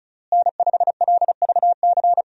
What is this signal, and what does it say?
Morse code, heard as a keyed tone, about 34 wpm, 700 Hz. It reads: N5LVC
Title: USA Call Signs - 5 Characters - Repeated 2x - 35wpm